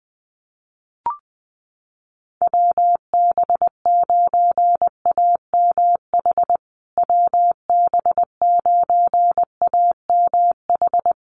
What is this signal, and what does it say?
Morse code, heard as a keyed tone, about 20 wpm, 700 Hz. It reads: WB9AMH WB9AMH
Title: USA Call Signs Repeated 3x 20wpm